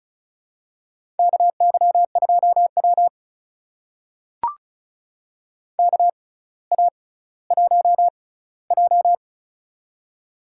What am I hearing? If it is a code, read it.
Morse code, heard as a keyed tone, about 33 wpm, spaced as for 18 wpm, 700 Hz. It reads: KY2W KA1J
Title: USA Call Signs - 6x Character Spacing - 35wpm